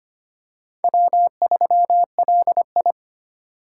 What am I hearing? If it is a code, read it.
Morse code, heard as a keyed tone, about 25 wpm, 700 Hz. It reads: W3LI
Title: USA Call Signs - 4 Characters - 2x Character Spacing - 25wpm